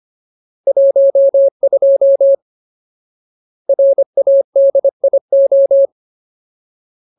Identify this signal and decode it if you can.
Morse code, heard as a keyed tone, about 25 wpm, 550 Hz. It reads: 12 RADIO
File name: Ultimate Word List / 25WPM Ultimate